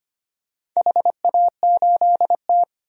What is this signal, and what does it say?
Morse code, heard as a keyed tone, about 25 wpm, 700 Hz. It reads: HA8T
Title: DXCC-100 Call Signs - 4 Characters - 25wpm